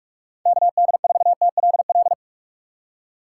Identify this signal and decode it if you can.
Morse code, heard as a keyed tone, about 46 wpm, 700 Hz. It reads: KD4TLL
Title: USA Call Signs - 4x Character Spacing - 45wpm